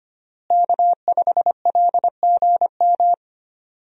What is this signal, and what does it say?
Morse code, heard as a keyed tone, about 25 wpm, 700 Hz. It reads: K5LGM
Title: USA Call Signs - 5 Characters - Repeated 2x - 25wpm